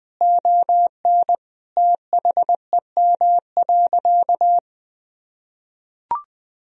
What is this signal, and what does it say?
Morse code, heard as a keyed tone, about 20 wpm, 700 Hz. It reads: ON THEM.
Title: Sentences from Top 1000 Words - Review Part 3 - 20wpm